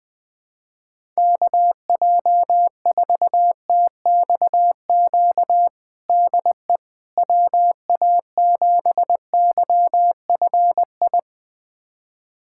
Morse code, heard as a keyed tone, about 20 wpm, 700 Hz. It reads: KJ4TXQ DE WA7YFI
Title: USA Callsign DE Callsign - 6 Characters - 20wpm